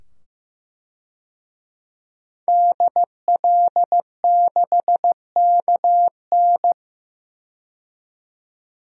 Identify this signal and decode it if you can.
Morse code, heard as a keyed tone, about 15 wpm, 700 Hz. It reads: DL6KN
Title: DXCC-100 Call Signs - 5 Characters - 15wpm